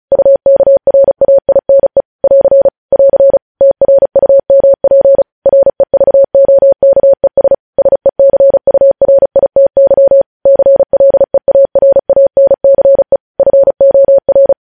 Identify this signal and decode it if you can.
Morse code, heard as a keyed tone, about 36 wpm, 550 Hz. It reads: UKRAINE <AR> <AR> TRUMP REVOKES SECURITY CLEARANCE FOR
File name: podcast / News headlines-1x-35x35